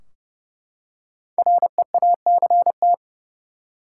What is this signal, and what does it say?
Morse code, heard as a keyed tone, about 30 wpm, 700 Hz. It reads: REACT
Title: Five Letter Words - 30wpm